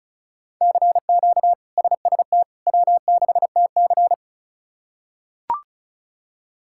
Morse code, heard as a keyed tone, about 36 wpm, 700 Hz. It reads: CQ SST W6TC
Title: K1USN SST QSO Practice - 35wpm